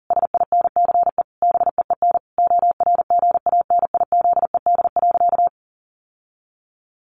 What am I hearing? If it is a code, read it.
Morse code, heard as a keyed tone, about 40 wpm, 700 Hz. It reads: SINCE BEEN ORGANIZED.
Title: Sentences from Top 1400 Words 40wpm